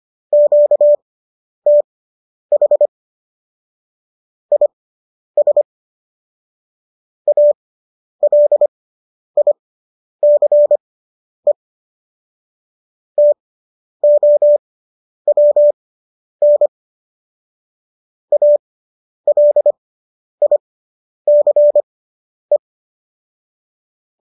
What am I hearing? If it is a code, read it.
Morse code, heard as a keyed tone, about 25 wpm, spaced as for 10 wpm, 590 Hz. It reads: QTH IS ALICE TOWN ALICE